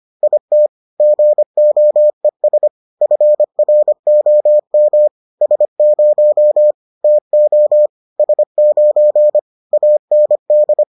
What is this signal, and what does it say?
Morse code, heard as a keyed tone, about 25 wpm, 600 Hz. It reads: IT GOES FROM S0 TO S9 AND